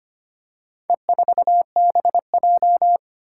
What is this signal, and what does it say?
Morse code, heard as a keyed tone, about 25 wpm, 700 Hz. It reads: E4BJ
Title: Four Challenging Letter-Number Combinations 25wpm